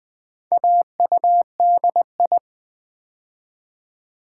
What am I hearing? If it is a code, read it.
Morse code, heard as a keyed tone, about 20 wpm, 700 Hz. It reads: AUDI
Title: Build A Word - 2 Letters At A Time - 8-Letter Words - 20wpm